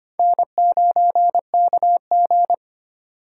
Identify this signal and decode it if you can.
Morse code, heard as a keyed tone, about 33 wpm, 700 Hz. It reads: N9KG